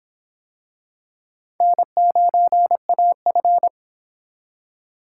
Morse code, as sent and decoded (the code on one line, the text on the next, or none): -. ----. .- ..-.
N9AF